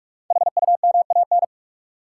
Morse code, heard as a keyed tone, about 46 wpm, 700 Hz. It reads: HUMAN
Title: Sentences from Top 600 Words - New Word Drill - 45wpm